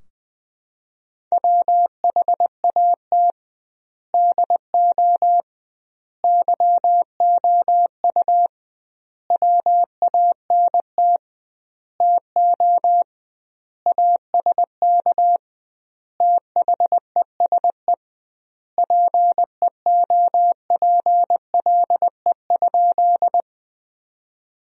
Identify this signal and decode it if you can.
Morse code, heard as a keyed tone, about 20 wpm, 700 Hz. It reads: WHAT DO YOU WANT TO ASK THESE PEOPLE?
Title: 2x Word-Spacing - Sentences from Top 200 Words - 20wpm